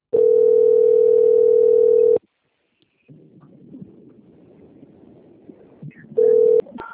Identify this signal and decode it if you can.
Ringback tone (North America), 440 + 480 Hz.